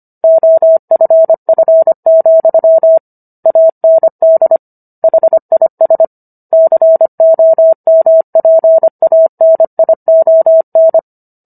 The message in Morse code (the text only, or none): OFF, AND HIS COMPANION